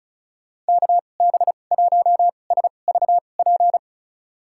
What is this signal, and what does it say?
Morse code, heard as a keyed tone, about 36 wpm, spaced as for 25 wpm, 700 Hz. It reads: KB1SVP
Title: USA Call Signs - 6 Characters - 2x Character Spacing - 35wpm